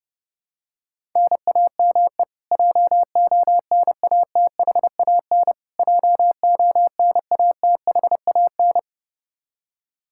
Morse code, heard as a keyed tone, about 32 wpm, 700 Hz. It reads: NAME JONATHAN JONATHAN
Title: Practice QSO - Exchange 1 - Beginning - 30wpm